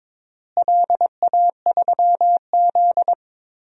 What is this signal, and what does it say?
Morse code, heard as a keyed tone, about 22 wpm, 700 Hz. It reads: LA3Z
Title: DXCC-100 Call Signs - 4 Characters - Repeated 3x - 22wpm